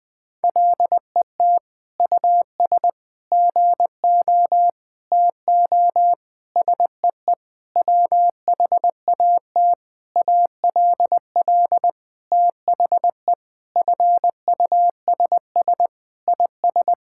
Morse code, heard as a keyed tone, about 20 wpm, 700 Hz. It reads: LET US GO TO SEE WHAT ALL THE FUSS IS
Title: Sentences with 8-Word Phrases - 20wpm